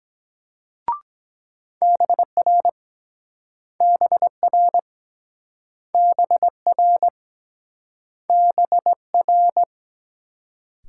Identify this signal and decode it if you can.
Morse code, heard as a keyed tone, about 24 wpm, 700 Hz. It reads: BR BR BR BR